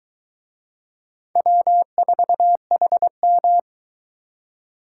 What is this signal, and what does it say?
Morse code, heard as a keyed tone, about 24 wpm, 700 Hz. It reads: W4HM